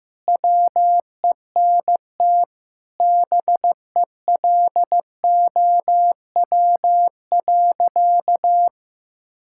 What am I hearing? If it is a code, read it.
Morse code, heard as a keyed tone, about 16 wpm, 700 Hz. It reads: WENT BELOW.
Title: Sentences from Top 1000 Words - Review Part 5 - 15wpm